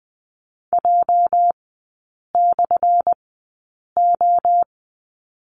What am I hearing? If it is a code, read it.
Morse code, heard as a keyed tone, about 20 wpm, spaced as for 8 wpm, 700 Hz. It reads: J/O
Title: 205 Lesson 33e - Letter j - Three Characters 20wpm